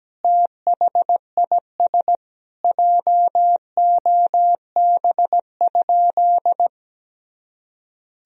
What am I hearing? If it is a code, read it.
Morse code, heard as a keyed tone, about 17 wpm, 700 Hz. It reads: THIS JOB?
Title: Sentences from Top 1000 Words - Review Part 3 - 17wpm